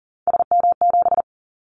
Morse code, heard as a keyed tone, about 38 wpm, 700 Hz. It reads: SM7
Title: Callsign Prefixes for Top 25 DXCC - 40wpm